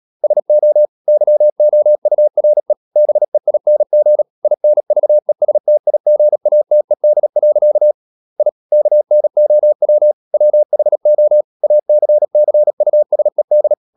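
Morse code, heard as a keyed tone, about 37 wpm, 600 Hz. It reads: SO YOURE BEING INVESTIGATED. I KNOW WHO ACCUSED
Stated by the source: Morse code practice at 35–39 wpm sending English text of Ken Follett's novel Never, from CW med Gnister.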